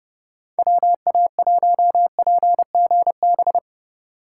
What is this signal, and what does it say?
Morse code, heard as a keyed tone, about 31 wpm, 700 Hz. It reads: WA1PGB
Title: Speed Racing - USA 2x3 Call Signs 25wpm